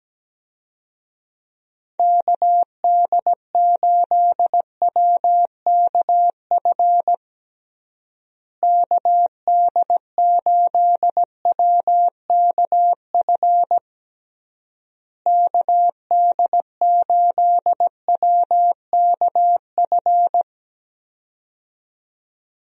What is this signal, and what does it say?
Morse code, heard as a keyed tone, about 17 wpm, 700 Hz. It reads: KD8WKF KD8WKF KD8WKF
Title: USA Call Signs - 6 Characters - Repeated 3x - 17wpm